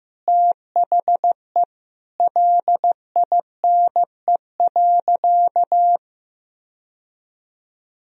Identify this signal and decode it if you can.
Morse code, heard as a keyed tone, about 16 wpm, 700 Hz. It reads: THE LINE.
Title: Sentences from Top 1000 Words - Review Part 1 - 15wpm